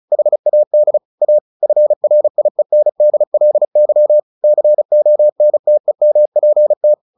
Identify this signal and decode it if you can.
Morse code, heard as a keyed tone, about 34 wpm, 600 Hz. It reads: HAD A FRIENDLY CONTEMPT
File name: the old man and the sea / the old man and the sea 1 35wpm